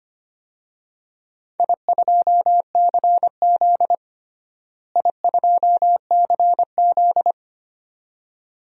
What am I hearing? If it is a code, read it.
Morse code, heard as a keyed tone, about 25 wpm, 700 Hz. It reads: I2CZ I2CZ
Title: DXCC-100 Call Signs - 4 Characters - Repeated 2x - 25wpm